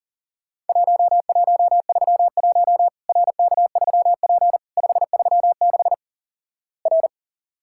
Morse code, heard as a keyed tone, about 43 wpm, 700 Hz. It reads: 1131 RK3P 536 R
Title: Worked All Europe DX Contest - QTC Exchange - 40wpm